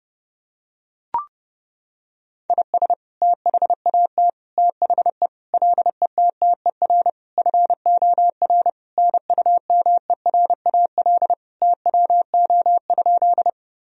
Morse code, heard as a keyed tone, about 30 wpm, 700 Hz. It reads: IS THAT THE LETTER FOR NUMERAL TWO?
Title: Sentences from Top 1000 Words - Review Part 2 - 30wpm